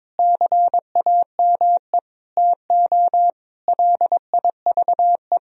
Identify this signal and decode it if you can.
Morse code, heard as a keyed tone, about 22 wpm, 700 Hz. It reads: CAME TO LIVE
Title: Sentences from Top 1000 Words - Review Part 3 - 22wpm